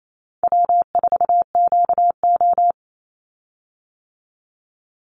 Morse code, heard as a keyed tone, about 28 wpm, 700 Hz. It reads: W4QO
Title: Call Signs - 28wpm